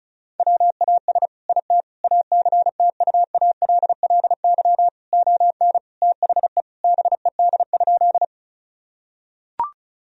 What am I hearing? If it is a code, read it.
Morse code, heard as a keyed tone, about 36 wpm, 700 Hz. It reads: WAS IT ACTUALLY ON THE BED?
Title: Sentences from Top 1000 Words - Review Part 4 - 35wpm